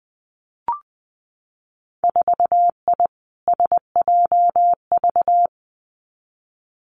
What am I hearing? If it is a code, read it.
Morse code, heard as a keyed tone, about 20 wpm, 700 Hz. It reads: 4I SJV
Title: Winter Field Day Exchanges Encore 20wpm